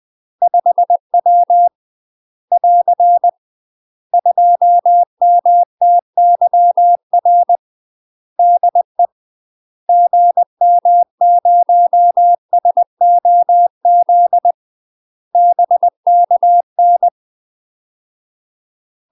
Morse code, heard as a keyed tone, about 20 wpm, 700 Hz. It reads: 5W <AR> 2MTYR DE GM0SOZ BKN
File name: UK MASTER QSOs TRAINING / 20WPM / UK MASTER QSOs 0063